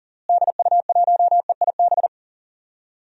Morse code, heard as a keyed tone, about 40 wpm, 700 Hz. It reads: DU1EIB
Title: DXCC-100 Call Signs - 6 Characters - Repeated 3x - 40wpm